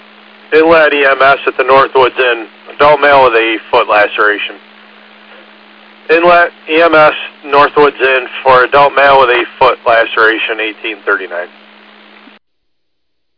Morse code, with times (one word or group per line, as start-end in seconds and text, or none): none